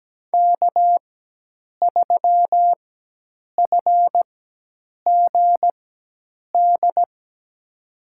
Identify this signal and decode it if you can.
Morse code, heard as a keyed tone, about 17 wpm, spaced as for 8 wpm, 700 Hz. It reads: K3FGD
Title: USA Call Signs - 5 Characters - 4x Character Spacing - 17wpm